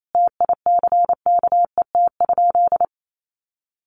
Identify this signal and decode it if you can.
Morse code, heard as a keyed tone, about 28 wpm, 700 Hz. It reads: TICKET?
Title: Sentences from Top 1400 Words 28wpm